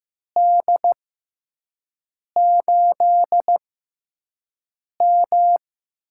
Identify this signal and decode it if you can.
Morse code, heard as a keyed tone, about 15 wpm, spaced as for 5 wpm, 700 Hz. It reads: D8M
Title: USA Call Signs - 6 Characters - 6x Character Spacing - 15wpm